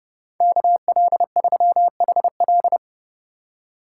Morse code, heard as a keyed tone, about 36 wpm, 700 Hz. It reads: KL3HL
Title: Speed Racing - USA 2x2 Call Signs 30wpm